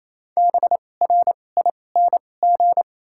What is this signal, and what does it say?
Morse code, heard as a keyed tone, about 28 wpm, spaced as for 20 wpm, 700 Hz. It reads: BRING
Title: 2x Character Spacing - Top 500 Words - 28wpm